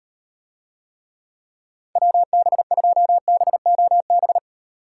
Morse code, heard as a keyed tone, about 38 wpm, 700 Hz. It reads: WB2BOB